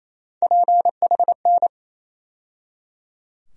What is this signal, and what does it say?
Morse code, heard as a keyed tone, about 27 wpm, 700 Hz. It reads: PHN
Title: USA Call Sign Postfixes - 3 Letters 28wpm